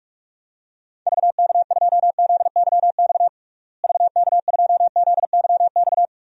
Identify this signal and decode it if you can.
Morse code, heard as a keyed tone, about 46 wpm, 700 Hz. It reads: VK2ZYX VK2ZYX
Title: DXCC-100 Call Signs - 6 Characters - Repeated 3x - 45wpm